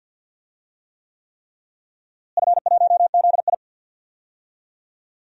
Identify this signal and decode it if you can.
Morse code, heard as a keyed tone, about 52 wpm, 700 Hz. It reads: F1ZI
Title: DXCC-100 Call Signs - 4 Characters - Repeated 3x - 50wpm